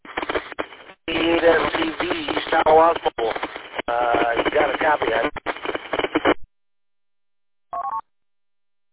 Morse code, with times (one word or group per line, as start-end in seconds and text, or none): none